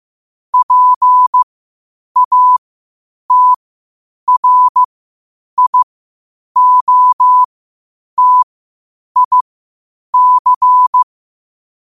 Morse code, heard as a keyed tone, about 15 wpm, spaced as for 8 wpm, 1000 Hz. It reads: PATRIOTIC
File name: qotdfiles / 8wpm